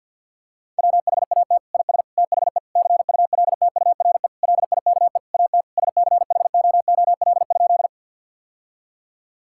Morse code, heard as a keyed tone, about 50 wpm, 700 Hz. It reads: WHAT IS THE CULTURE LIKE AT SCHOOL?
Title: Sentences from Top 1000 Words - Review Part 6 - 50wpm